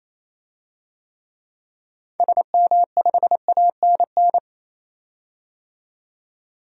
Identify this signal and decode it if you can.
Morse code, heard as a keyed tone, about 28 wpm, 700 Hz. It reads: SM5ANN
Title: DXCC-100 Call Signs - 6 Characters - 28wpm